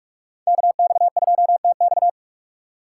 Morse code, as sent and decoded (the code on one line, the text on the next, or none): -.- -..- ..--- - -..-
KX2TX